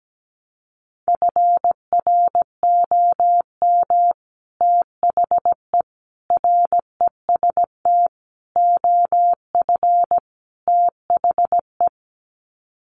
Morse code, heard as a keyed tone, about 17 wpm, 700 Hz. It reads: FROM THE REST OF THE
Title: Sets of 5 Words 17wpm